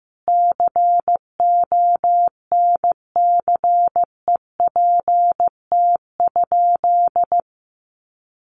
Morse code, heard as a keyed tone, about 15 wpm, 700 Hz. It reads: CONCEPT?
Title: Sentences from Top 1100 Words 15wpm